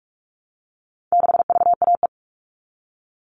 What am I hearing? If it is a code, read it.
Morse code, heard as a keyed tone, about 44 wpm, 700 Hz. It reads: 6VAE
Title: All Cut Numbers - 4-Digit 45wpm